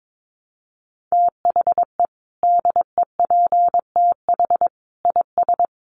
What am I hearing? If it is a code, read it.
Morse code, heard as a keyed tone, about 22 wpm, 700 Hz. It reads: THE DEPTH IS
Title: Sentences from Top 1200 Words 22wpm